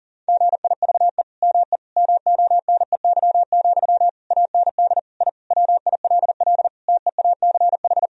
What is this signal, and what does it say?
Morse code, heard as a keyed tone, about 40 wpm, 700 Hz. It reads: GIVE ME MONEY, AND I WILL TEACH
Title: Sentences from Top 1000 Words - Review Part 2 - 40wpm